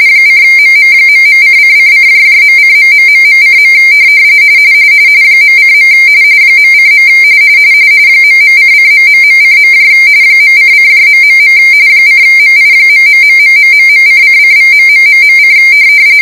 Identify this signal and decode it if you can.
RTTY, about 45.45 baud. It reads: :MM..::::.:....:. .:. .:. ..::...:. .::... :MI.......:::=....: